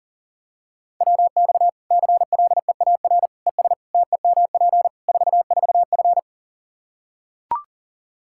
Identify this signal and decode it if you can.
Morse code, heard as a keyed tone, about 40 wpm, 700 Hz. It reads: WX CLEAR ES TEMP 44F
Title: Practice QSO - Exchange 2 - Info - 40wpm